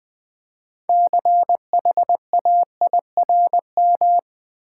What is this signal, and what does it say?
Morse code, heard as a keyed tone, about 20 wpm, 700 Hz. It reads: CHAIRM
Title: Build A Word - 2 Letters At A Time - 8-Letter Words - 20wpm